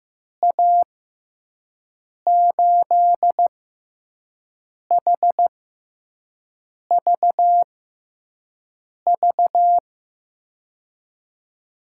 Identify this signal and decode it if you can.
Morse code, heard as a keyed tone, about 15 wpm, spaced as for 5 wpm, 700 Hz. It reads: A8HVV
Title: USA Call Signs - 6 Characters - 6x Character Spacing - 15wpm